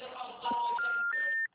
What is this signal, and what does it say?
Special information tone, with noise.
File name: out- -2020- - - .64668